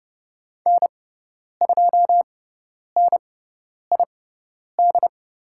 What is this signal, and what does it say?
Morse code, heard as a keyed tone, about 30 wpm, spaced as for 10 wpm, 700 Hz. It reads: N2NID
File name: Lesson 16g - Letter c - Call Signs 30wpm